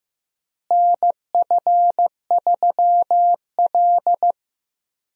Morse code, heard as a keyed tone, about 15 wpm, 700 Hz. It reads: NF3L
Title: USA Call Signs - 4 Characters - Repeated 2x - 15wpm